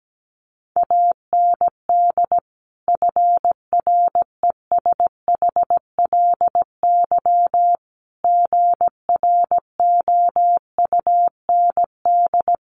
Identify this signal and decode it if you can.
Morse code, heard as a keyed tone, about 17 wpm, 690 Hz. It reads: AND FRESHLY GROUND
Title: Sets of 5 Words Encore 17wpm